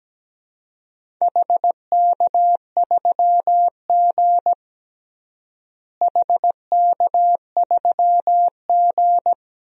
Morse code, heard as a keyed tone, about 17 wpm, 700 Hz. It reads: HK3G HK3G
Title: DXCC-100 Call Signs - 4 Characters - Repeated 3x - 17wpm